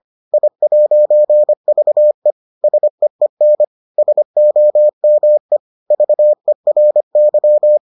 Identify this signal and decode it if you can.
Morse code, heard as a keyed tone, about 25 wpm, 600 Hz. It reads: I'VE SEEN SOME VERY